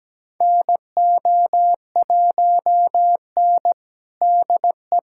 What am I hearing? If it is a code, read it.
Morse code, heard as a keyed tone, about 17 wpm, 700 Hz. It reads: NO1N DE
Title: USA Callsign DE Callsign - 4 Characters - 17wpm